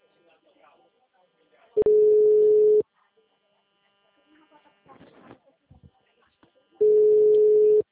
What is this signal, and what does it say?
Ringback tone (Europe, 425 Hz), with speech.